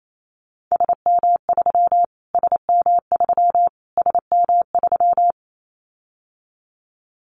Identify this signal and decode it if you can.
Morse code, heard as a keyed tone, about 28 wpm, 700 Hz. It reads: SM3 SM3 SM3
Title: Callsign Prefixes for Top 25 DXCC - Repeated 3x - 28wpm